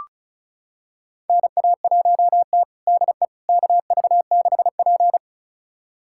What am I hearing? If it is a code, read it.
Morse code, heard as a keyed tone, about 36 wpm, 700 Hz. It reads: NA1T DE KV6P
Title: USA Callsign DE Callsign - 4 Characters - 35wpm